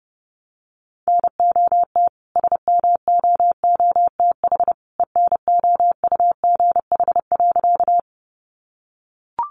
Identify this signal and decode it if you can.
Morse code, heard as a keyed tone, about 30 wpm, 700 Hz. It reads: NOT SMOOTH ENOUGH.
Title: Sentences from Top 1400 Words 30wpm